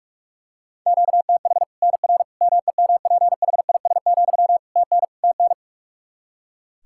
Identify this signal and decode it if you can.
Morse code, heard as a keyed tone, about 46 wpm, 700 Hz. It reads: QTH NR MEMPHIS, TN TN
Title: Practice QSO - Exchange 1 - Beginning - 45wpm